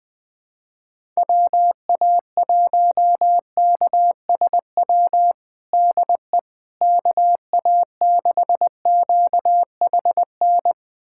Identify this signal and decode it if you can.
Morse code, heard as a keyed tone, about 20 wpm, 700 Hz. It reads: WA1KSW DE KA6QHN
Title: USA Callsign DE Callsign - 6 Characters - 20wpm